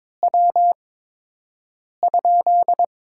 Morse code, heard as a keyed tone, about 22 wpm, 700 Hz. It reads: W ?
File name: Top 20 QSO Elements - 8x Character Spacing - 22wpm